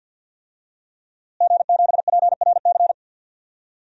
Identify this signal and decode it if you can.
Morse code, heard as a keyed tone, about 52 wpm, 700 Hz. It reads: G7PRC
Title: DXCC-100 Call Signs - 5 Characters - Repeated 2x - 50wpm